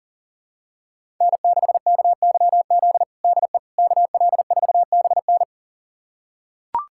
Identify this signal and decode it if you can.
Morse code, heard as a keyed tone, about 40 wpm, 700 Hz. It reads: N6KYZ DE KL4BN
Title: USA Callsign DE Callsign - 5 Characters - 40wpm